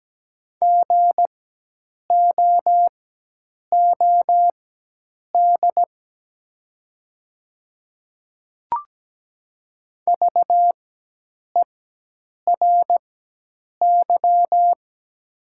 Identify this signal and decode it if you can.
Morse code, heard as a keyed tone, about 17 wpm, spaced as for 8 wpm, 700 Hz. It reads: GOOD VERY
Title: 4x Character Spacing - Top 500 Words - 17wpm